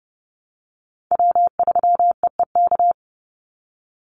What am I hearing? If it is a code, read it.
Morse code, heard as a keyed tone, about 30 wpm, 700 Hz. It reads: W3EEK
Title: Call Signs - 30wpm